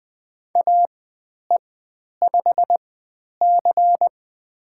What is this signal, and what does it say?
Morse code, heard as a keyed tone, about 20 wpm, spaced as for 10 wpm, 700 Hz. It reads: AE5C
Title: 102 Lesson 17g - Letter u - Call Signs 20wpm